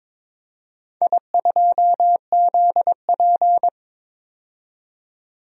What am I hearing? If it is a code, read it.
Morse code, heard as a keyed tone, about 22 wpm, 700 Hz. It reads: I2ZP
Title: DXCC-100 Call Signs - 4 Characters - 22wpm